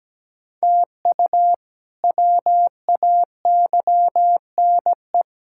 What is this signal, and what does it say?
Morse code, heard as a keyed tone, about 17 wpm, 700 Hz. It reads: TU WAYNE
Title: K1USN SST QSO Practice - 17wpm